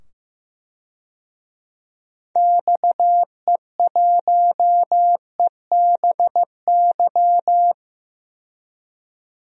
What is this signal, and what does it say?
Morse code, heard as a keyed tone, about 15 wpm, 700 Hz. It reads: XE1EBY
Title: DXCC-100 Call Signs - 6 Characters - 15wpm